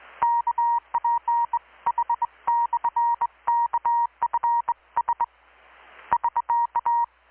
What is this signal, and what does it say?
Morse code, heard as a keyed tone, about 20 wpm, 940 Hz. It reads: KPH/KFS <SK>
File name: blog / swl / 6-abril-2019-12695-USB-KPS